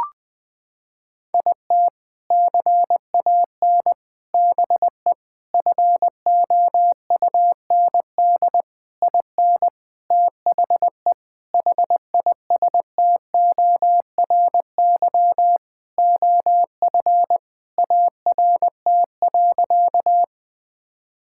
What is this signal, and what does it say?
Morse code, heard as a keyed tone, about 20 wpm, 700 Hz. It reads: IT CAN BE FOUND IN THE HISTORY OF ART.
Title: Sentences with 4-Word Phrases - 20wpm